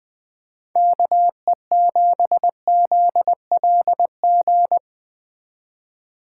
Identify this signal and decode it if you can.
Morse code, heard as a keyed tone, about 20 wpm, 700 Hz. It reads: KE7ZLG
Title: Speed Racing - USA Call Signs 17wpm